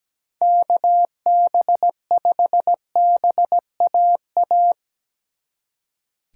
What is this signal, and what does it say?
Morse code, heard as a keyed tone, about 20 wpm, 700 Hz. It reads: KB5BAA
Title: Speed Racing - USA 2x3 Call Signs 17wpm